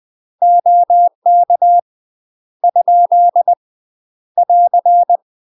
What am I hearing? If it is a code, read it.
Morse code, heard as a keyed tone, about 20 wpm, 700 Hz. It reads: OK ? <AR>
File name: UK MASTER QSOs TRAINING / 20WPM / UK MASTER QSOs 0057